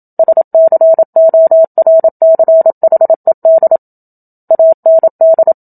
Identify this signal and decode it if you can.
Morse code, heard as a keyed tone, about 27 wpm, 650 Hz. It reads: SCORCHED AND
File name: Book Two / 01-Chapter 1 - 27 WPM